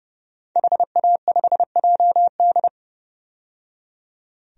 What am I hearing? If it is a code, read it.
Morse code, heard as a keyed tone, about 30 wpm, 700 Hz. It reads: HA5JD
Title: DXCC-100 Call Signs - 5 Characters - 30wpm